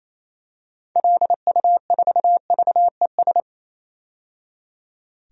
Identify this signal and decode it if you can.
Morse code, heard as a keyed tone, about 28 wpm, 700 Hz. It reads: LU4VES
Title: DXCC-100 Call Signs - 6 Characters - 28wpm